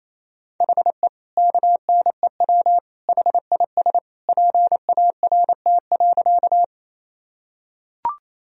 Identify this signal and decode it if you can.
Morse code, heard as a keyed tone, about 28 wpm, 700 Hz. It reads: HE KNEW HIS PART.
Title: Sentences from Top 1000 Words - Review Part 1 - 28wpm